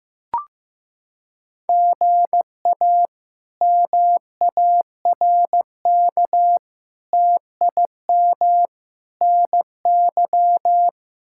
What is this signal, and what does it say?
Morse code, heard as a keyed tone, about 15 wpm, 700 Hz. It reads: GA MARK TIM NY
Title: K1USN SST QSO Practice - 15wpm